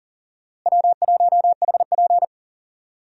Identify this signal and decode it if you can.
Morse code, heard as a keyed tone, about 39 wpm, 700 Hz. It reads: W1HP
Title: USA Call Signs - 4 Characters - Repeated 3x - 40wpm